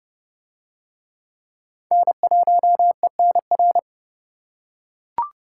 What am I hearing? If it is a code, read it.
Morse code, heard as a keyed tone, about 29 wpm, 700 Hz. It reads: N1ENR